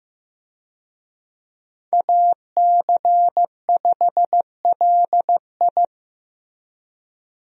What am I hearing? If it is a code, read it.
Morse code, heard as a keyed tone, about 15 wpm, 700 Hz. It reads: AC5LI
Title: USA Call Signs - 5 Characters - Repeated 2x - 15wpm